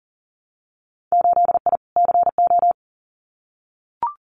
Morse code, heard as a keyed tone, about 40 wpm, 700 Hz. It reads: 8I CO
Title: Winter Field Day Exchanges Encore 40wpm